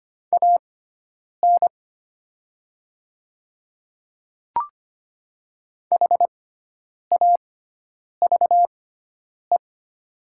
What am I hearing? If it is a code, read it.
Morse code, heard as a keyed tone, about 25 wpm, spaced as for 9 wpm, 700 Hz. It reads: AN HAVE
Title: 6x Character Spacing - Top 100 Words - 25wpm